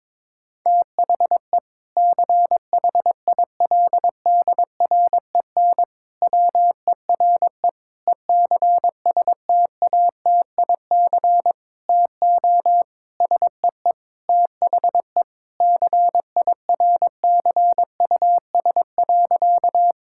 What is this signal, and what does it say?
Morse code, heard as a keyed tone, about 22 wpm, 700 Hz. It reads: THE CHILDREN WERE ECSTATIC TO SEE THE CIRCUS.
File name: Difficult Sentences with Hints - 22wpm